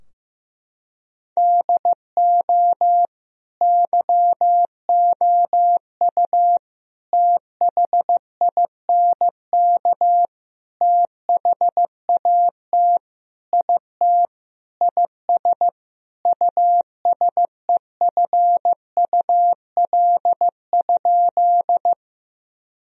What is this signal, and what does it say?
Morse code, heard as a keyed tone, about 15 wpm, 700 Hz. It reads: DO YOU THINK THAT IT IS USEFUL?
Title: Sentences from Top 1000 Words - Review Part 4 - 15wpm